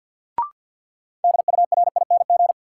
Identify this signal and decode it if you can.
Morse code, heard as a keyed tone, about 47 wpm, 700 Hz. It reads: DURING
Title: Sets of 2 Words - New Word Drill - 50wpm